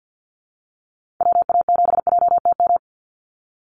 Morse code, heard as a keyed tone, about 50 wpm, 700 Hz. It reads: WA7JTM
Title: NA SOTA Activator Callsigns 50wpm